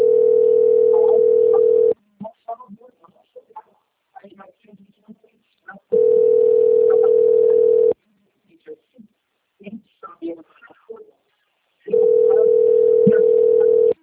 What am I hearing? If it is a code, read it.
Ringback tone (North America), 440 + 480 Hz, with speech.
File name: out- - - - - .26347